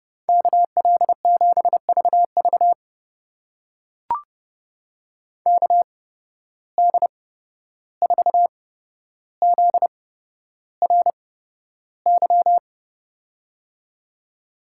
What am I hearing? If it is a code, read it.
Morse code, heard as a keyed tone, about 30 wpm, spaced as for 12 wpm, 700 Hz. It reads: KL7VV KD4ZRY
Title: USA Call Signs - 8x Character Spacing - 30wpm